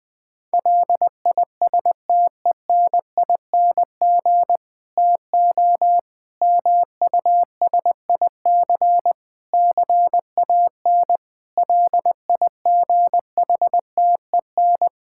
Morse code, heard as a keyed tone, about 20 wpm, 700 Hz. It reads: LISTENING TO MUSIC CAN LIGHTEN
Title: Difficult Sentences with Hints - 20wpm